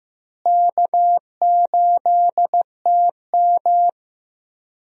Morse code, heard as a keyed tone, about 15 wpm, 700 Hz. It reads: K8TM
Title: USA Call Signs - 4 Characters - Repeated 2x - 15wpm